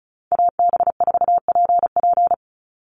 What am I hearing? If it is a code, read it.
Morse code, heard as a keyed tone, about 34 wpm, 700 Hz. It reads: AB4PP
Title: NA SOTA Chaser Callsigns 35wpm